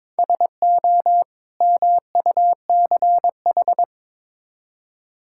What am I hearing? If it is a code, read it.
Morse code, heard as a keyed tone, about 22 wpm, 700 Hz. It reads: SO MUCH
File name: practice / 22wpm / Build A Phrase - 5-Word Phrases - 22wpm